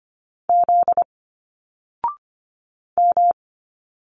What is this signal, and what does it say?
Morse code, heard as a keyed tone, about 25 wpm, 700 Hz. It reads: Z M
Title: Single Letters 25wpm